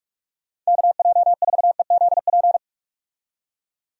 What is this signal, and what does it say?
Morse code, heard as a keyed tone, about 46 wpm, 700 Hz. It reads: KJ4EZP
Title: USA Call Signs - 6 Characters - 45wpm